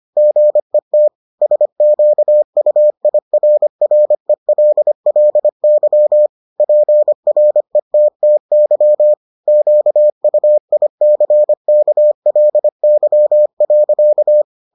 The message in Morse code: GET SQUIRRELLY PRETTY QUICKLY.

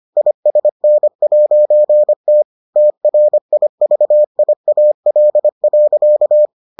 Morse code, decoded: ISN'T TRIVIAL.

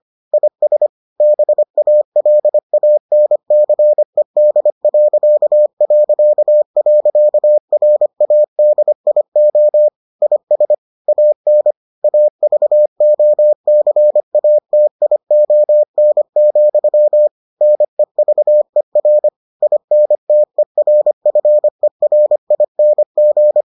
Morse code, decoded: IS BALANCED...RADIO IS AN AVOCATION, NEVER INTERFERING

